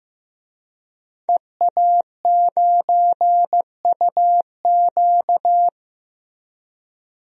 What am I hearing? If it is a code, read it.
Morse code, heard as a keyed tone, about 15 wpm, 700 Hz. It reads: EA9UQ
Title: DXCC-100 Call Signs - 5 Characters - 15wpm